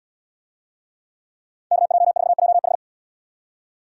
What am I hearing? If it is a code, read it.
Morse code, heard as a keyed tone, about 77 wpm, 700 Hz. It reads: KJ4JH